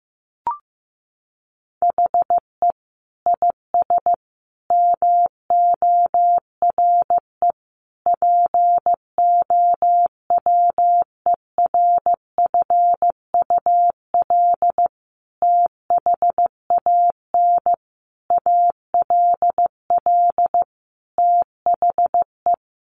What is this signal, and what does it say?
Morse code, heard as a keyed tone, about 15 wpm, 700 Hz. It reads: HE IS MORE POWERFUL THAN ALL THE
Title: Sentences from Top 1300 Words 15wpm